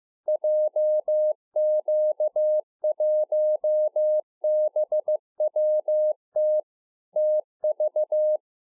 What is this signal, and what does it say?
Morse code, heard as a keyed tone, about 15 wpm, 610 Hz. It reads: JQ1BWT TV